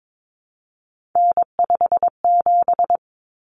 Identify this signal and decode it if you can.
Morse code, heard as a keyed tone, about 22 wpm, 700 Hz. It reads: N57
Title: Common Cut Numbers - 3-Digit 22wpm